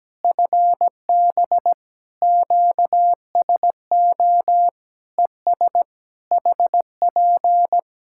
Morse code, heard as a keyed tone, about 17 wpm, 700 Hz. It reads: FB QSO ES HP
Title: Practice QSO - Exchange 4 - Ending - 17wpm